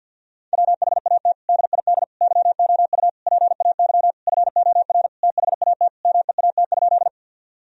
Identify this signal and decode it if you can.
Morse code, heard as a keyed tone, about 47 wpm, 700 Hz. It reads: WHAT DID YOU PAY FOR THAT MEAT?
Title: Sentences from Top 1000 Words - Review Part 3 - 50wpm